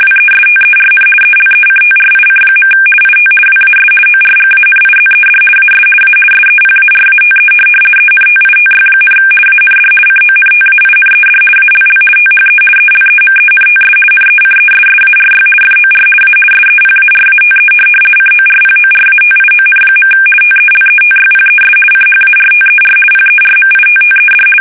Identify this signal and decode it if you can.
RTTY, about 50 baud. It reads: WESTERN ELECTRIC. IT WAS ORGANIZED AS A SEPARATE ENTITY, AND CONTAINED ALL THE ELEMENTS NECESSARY FOR A SEPARATE CORPORATION. TELETYPES CHARTER PERMITTED
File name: fsk850-50B / teletype-corp